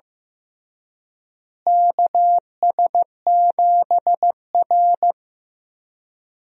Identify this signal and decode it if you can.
Morse code, heard as a keyed tone, about 15 wpm, 700 Hz. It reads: KS7R
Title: USA Call Signs - 4 Characters - Repeated 3x - 15wpm